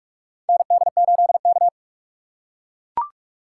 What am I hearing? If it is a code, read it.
Morse code, heard as a keyed tone, about 42 wpm, 700 Hz. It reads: ND8K
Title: USA Call Signs - 4x Character Spacing - 45wpm